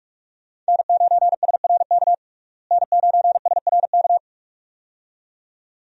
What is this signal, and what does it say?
Morse code, heard as a keyed tone, about 47 wpm, 700 Hz. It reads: N9SRK N9SRK
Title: USA Call Signs - 5 Characters - Repeated 2x - 45wpm